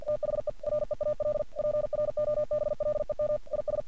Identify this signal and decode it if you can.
Morse code, heard as a keyed tone, about 48 wpm, 600 Hz. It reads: THE REAL PROBLEM IS
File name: archived / CRSnet